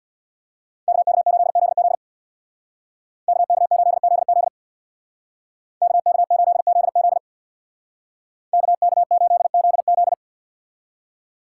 Morse code, heard as a keyed tone, about 62 wpm, 700 Hz. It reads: KK8ZZ KK8ZZ KK8ZZ KK8ZZ
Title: Speed Racing - USA 2x2 Call Signs 50wpm